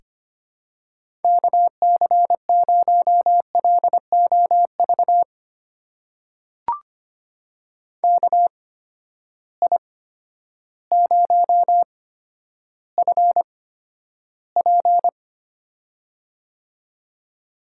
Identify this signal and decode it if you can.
Morse code, heard as a keyed tone, about 25 wpm, spaced as for 11 wpm, 700 Hz. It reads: KC0LOV KI0FP
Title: USA Call Signs - 8x Character Spacing - 25wpm